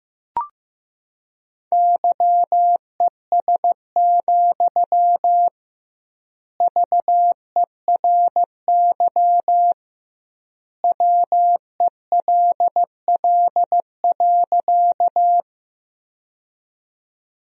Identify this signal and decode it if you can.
Morse code, heard as a keyed tone, about 15 wpm, 700 Hz. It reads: YES, VERY WELL.
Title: 2x Word-Spacing - Sentences from Top 500 Words - 15wpm